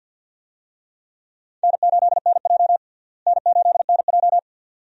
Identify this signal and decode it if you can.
Morse code, heard as a keyed tone, about 52 wpm, 700 Hz. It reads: N8NJ N8NJ
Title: USA Call Signs - 4 Characters - Repeated 2x - 50wpm